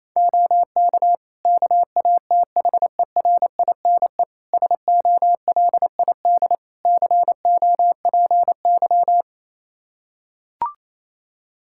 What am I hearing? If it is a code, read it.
Morse code, heard as a keyed tone, about 28 wpm, 700 Hz. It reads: OK KATHERINE SOLID COPY
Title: K1USN SST QSO Practice 28wpm